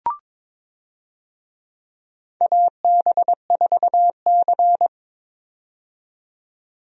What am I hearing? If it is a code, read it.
Morse code, heard as a keyed tone, about 22 wpm, 700 Hz. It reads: AB4C
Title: USA Call Signs - 4 Characters - 22wpm